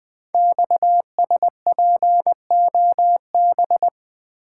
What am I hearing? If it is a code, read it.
Morse code, heard as a keyed tone, about 20 wpm, 700 Hz. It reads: XSPOB
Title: Code Groups - 5 Letters - 20wpm